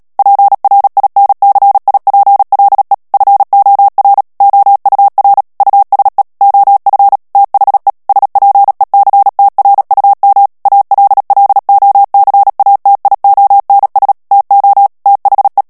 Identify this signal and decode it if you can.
Morse code, heard as a keyed tone, about 37 wpm, 800 Hz. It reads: PRINCIPLE FOR OUR USE OF THE SPECTRUM ALLOCATIONS TO THE
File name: morse mp3 / ANGLAIS texte clair 3 vitesse 35 mots minute